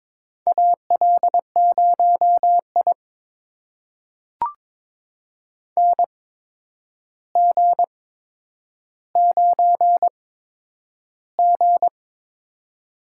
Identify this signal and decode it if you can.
Morse code, heard as a keyed tone, about 22 wpm, spaced as for 9 wpm, 700 Hz. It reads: AL0I NG9G
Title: USA Call Signs - 4 Characters - 8x Character Spacing - 22wpm